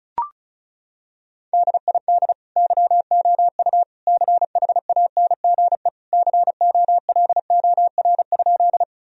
Morse code, heard as a keyed tone, about 36 wpm, 700 Hz. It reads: DID YOU CHANGE COLOR?
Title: Sentences from Top 1000 Words - Review Part 2 - 35wpm